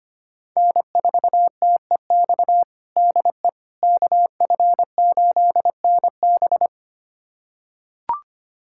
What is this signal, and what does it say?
Morse code, heard as a keyed tone, about 25 wpm, 700 Hz. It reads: N4TEX DE KF8NB
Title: USA Callsign DE Callsign - 5 Characters - 25wpm